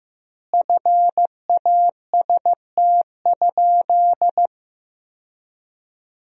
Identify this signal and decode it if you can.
Morse code, heard as a keyed tone, about 15 wpm, 700 Hz. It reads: FAST?
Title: 2x Word-Spacing - Sentences from Top 400 Words - 15wpm